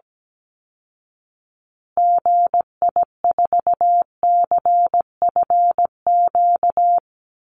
Morse code, heard as a keyed tone, about 17 wpm, 700 Hz. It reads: GI4CFQ
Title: Call Signs Repeated 3x - 17wpm